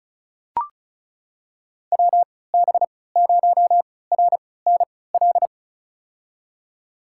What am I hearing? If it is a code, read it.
Morse code, heard as a keyed tone, about 36 wpm, spaced as for 20 wpm, 700 Hz. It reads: WB0RNL
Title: USA Call Signs - 3x Character Spacing - 35wpm